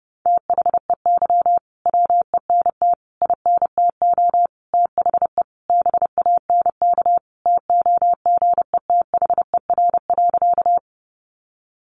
Morse code, heard as a keyed tone, about 30 wpm, 700 Hz. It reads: THEY WENT INTO THE BANK TOGETHER.
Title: Sentences from Top 900 Words 30wpm